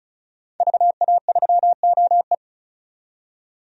Morse code, heard as a keyed tone, about 34 wpm, 700 Hz. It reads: VA3OE
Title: DXCC-100 Call Signs - 5 Characters - 35wpm